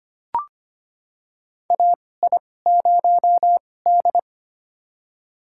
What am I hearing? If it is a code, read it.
Morse code, heard as a keyed tone, about 25 wpm, spaced as for 18 wpm, 700 Hz. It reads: AI0D
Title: USA Call Signs - 4 Characters - 2x Character Spacing - 25wpm